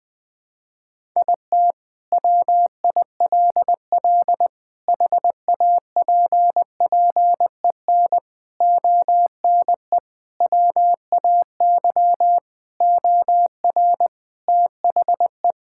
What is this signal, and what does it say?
Morse code, heard as a keyed tone, about 20 wpm, 700 Hz. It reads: IT WILL HAPPEN ONE WAY OR THE
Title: Sentences with 5-Word Phrases - 20wpm